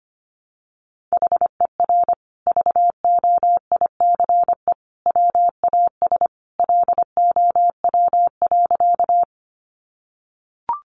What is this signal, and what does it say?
Morse code, heard as a keyed tone, about 25 wpm, 700 Hz. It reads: HER VOICE WAS LOW.
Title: Sentences from Top 500 Words 25wpm